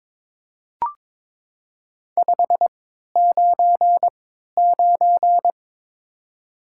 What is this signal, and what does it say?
Morse code, heard as a keyed tone, about 22 wpm, spaced as for 13 wpm, 700 Hz. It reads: 599
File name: Top 90 QSO Elements - 3x Character Spacing - 22wpm